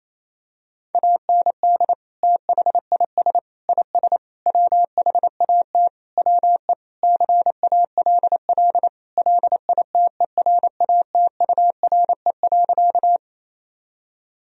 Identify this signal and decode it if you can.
Morse code, heard as a keyed tone, about 27 wpm, 700 Hz. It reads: AND THIS IS WHAT WE CALL LITERATURE.
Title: Sentences from Top 1000 Words - Review Part 3 - 28wpm